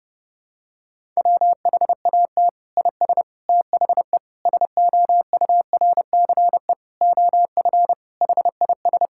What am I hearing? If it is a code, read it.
Morse code, heard as a keyed tone, about 30 wpm, 700 Hz. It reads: WHAT IS THE SOURCE OF HIS
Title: Sentences from Top 1000 Words - Review Part 5 - 30wpm